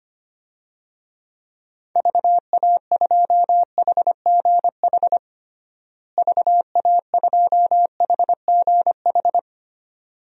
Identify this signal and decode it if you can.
Morse code, heard as a keyed tone, about 25 wpm, 700 Hz. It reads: VA2HGH VA2HGH
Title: DXCC-100 Call Signs - 6 Characters - Repeated 2x - 25wpm